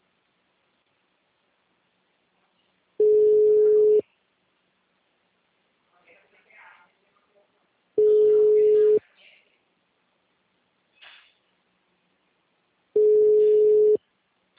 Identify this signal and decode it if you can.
Ringback tone (Europe, 425 Hz), with speech.